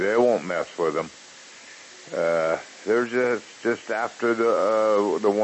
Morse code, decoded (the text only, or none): none